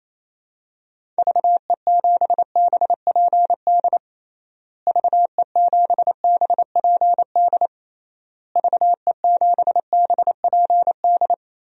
Morse code, heard as a keyed tone, about 28 wpm, 700 Hz. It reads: VE7BPD VE7BPD VE7BPD
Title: DXCC-100 Call Signs - 6 Characters - Repeated 3x - 28wpm